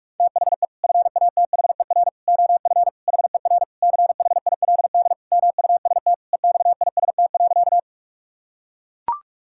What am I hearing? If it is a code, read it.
Morse code, heard as a keyed tone, about 42 wpm, 700 Hz. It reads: THE FATHER OF HER CHILD MUST EXIST.
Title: Sentences from Top 1000 Words - Review Part 5 - 45wpm